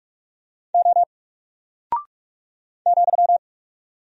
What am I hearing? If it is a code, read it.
Morse code, heard as a keyed tone, about 44 wpm, 700 Hz. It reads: O ,
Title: Single Character (Letter:Number:Punctuation:Prosign) 45wpm